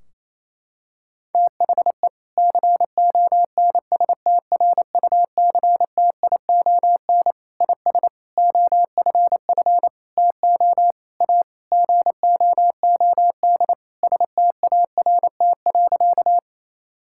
Morse code, heard as a keyed tone, about 28 wpm, 700 Hz. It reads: THE CONSTRUCTION IS OFF TO A GOOD START.
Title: Sentences from Top 1600 Words 28wpm